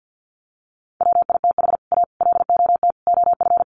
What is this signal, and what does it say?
Morse code, heard as a keyed tone, about 50 wpm, 700 Hz. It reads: WITH A LOT OF
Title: Sets of 4 Words 50wpm